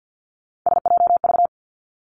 Morse code, heard as a keyed tone, about 50 wpm, 700 Hz. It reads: SJV
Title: ARRL Sections 50wpm